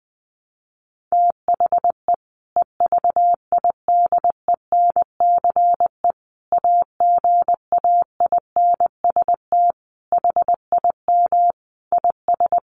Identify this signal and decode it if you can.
Morse code, heard as a keyed tone, about 20 wpm, 700 Hz. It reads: THE EVIDENCE AGAINST HIM IS
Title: Sentences from Top 1300 Words 20wpm